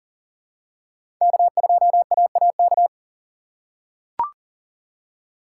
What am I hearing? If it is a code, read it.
Morse code, heard as a keyed tone, about 38 wpm, 700 Hz. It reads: K2AAK
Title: USA Call Signs - 5 Characters - 4x Character Spacing - 40wpm